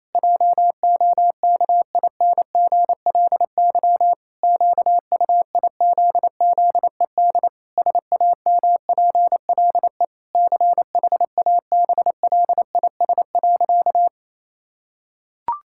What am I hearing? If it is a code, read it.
Morse code, heard as a keyed tone, about 28 wpm, 700 Hz. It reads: JOKINGLY QUIZZED SAMPLE CHABLIS.
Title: Pangram Challenge v2 - 28wpm